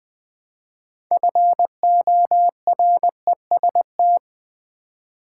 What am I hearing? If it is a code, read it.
Morse code, heard as a keyed tone, about 20 wpm, 700 Hz. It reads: FOREST
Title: Build A Word - 3 Letters At A Time - 6-Letter Words - 20wpm